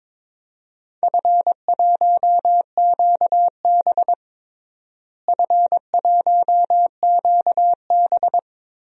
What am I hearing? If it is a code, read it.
Morse code, heard as a keyed tone, about 22 wpm, 700 Hz. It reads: F1QB F1QB
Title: DXCC-100 Call Signs - 4 Characters - Repeated 2x - 22wpm